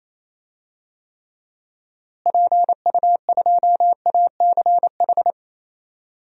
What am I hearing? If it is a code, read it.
Morse code, heard as a keyed tone, about 28 wpm, 700 Hz. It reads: PU2ACH
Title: DXCC-100 Call Signs - 6 Characters - Repeated 3x - 28wpm